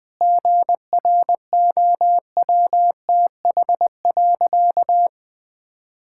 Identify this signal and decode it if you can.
Morse code, heard as a keyed tone, about 21 wpm, 700 Hz. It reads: GROWTH.
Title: Sentences from Top 1000 Words - Review Part 5 - 20wpm